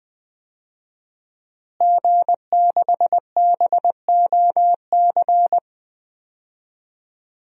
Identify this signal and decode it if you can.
Morse code, heard as a keyed tone, about 21 wpm, 700 Hz. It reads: G6BOC
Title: DXCC-100 Call Signs - 5 Characters - Repeated 3x - 20wpm